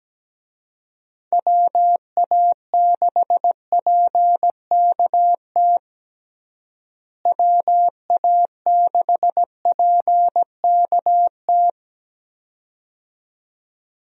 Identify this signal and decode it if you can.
Morse code, heard as a keyed tone, about 17 wpm, 700 Hz. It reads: WA6PKT WA6PKT
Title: USA Call Signs - 6 Characters - Repeated 2x - 17wpm